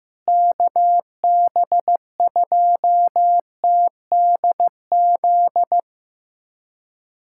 Morse code, heard as a keyed tone, about 15 wpm, 700 Hz. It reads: KB2TDZ
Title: USA Callsign DE Callsign - 6 Characters - 15wpm